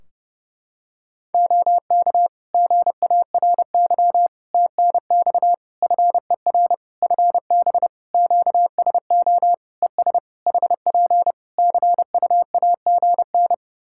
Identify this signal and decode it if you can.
Morse code, heard as a keyed tone, about 30 wpm, 700 Hz. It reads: OK GARY TNX FER FB QSO ES HP CUAGN
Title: Practice QSO - Exchange 4 - Ending - 30wpm